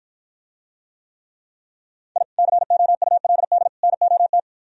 Morse code, heard as a keyed tone, about 52 wpm, 700 Hz. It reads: I COULD NOT